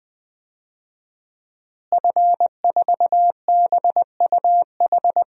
Morse code, heard as a keyed tone, about 20 wpm, 700 Hz. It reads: F4BUH
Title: DXCC-100 Call Signs - 5 Characters - Repeated 3x - 20wpm